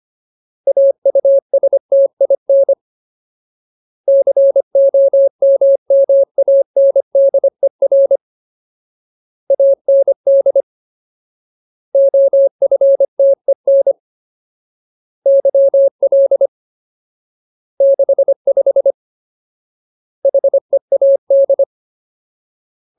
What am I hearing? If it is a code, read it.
Morse code, heard as a keyed tone, about 25 wpm, 550 Hz. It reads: AUSTIN COMMANDER AND OFTEN YL 65 HEAD
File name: Ultimate Word List / 25WPM Ultimate